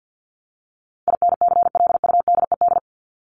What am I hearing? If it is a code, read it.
Morse code, heard as a keyed tone, about 50 wpm, 700 Hz. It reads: INCLUDED
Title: ED Suffix - Top 100 Words with Base Words 50wpm